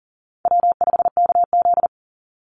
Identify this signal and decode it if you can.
Morse code, heard as a keyed tone, about 40 wpm, 690 Hz. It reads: W5KZ
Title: USA Call Signs 40wpm